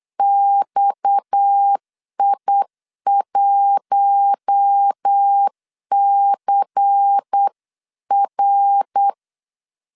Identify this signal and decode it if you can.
Morse code, heard as a keyed tone, about 8 wpm, 800 Hz. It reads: XI1CR
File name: LEZIONI / 6 PKX0 / 6 PKX 8wpm 3041